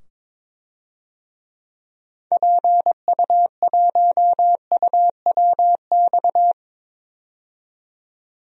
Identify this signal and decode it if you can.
Morse code, heard as a keyed tone, about 22 wpm, 700 Hz. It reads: PU1UWX
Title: DXCC-100 Call Signs - 6 Characters - 22wpm